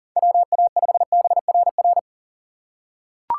Morse code, heard as a keyed tone, about 40 wpm, 700 Hz. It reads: WA5BRR
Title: USA Call Signs - 6 Characters - 2x Character Spacing - 40wpm